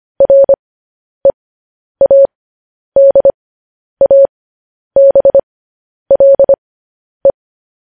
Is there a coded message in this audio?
Yes: Morse code from 0.2 to 7.3 s, heard as a keyed tone, about 25 wpm, spaced as for 10 wpm, 550 Hz. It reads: READABLE